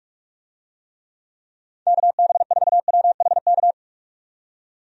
Morse code, heard as a keyed tone, about 46 wpm, 700 Hz. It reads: KB4WHK
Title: USA Call Signs - 6 Characters - 45wpm